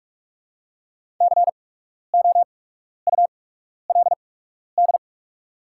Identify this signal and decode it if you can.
Morse code, heard as a keyed tone, about 44 wpm, spaced as for 12 wpm, 700 Hz. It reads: COULD
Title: 8x Character Spacing - Top 300 Words - 45wpm